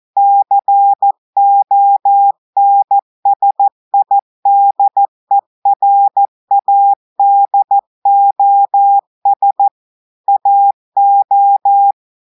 Morse code, heard as a keyed tone, about 14 wpm, 800 Hz. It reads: CONSIDERADOS AO
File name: audios / portugais / portugais texte en clair 2 vitesse 14 mots minutes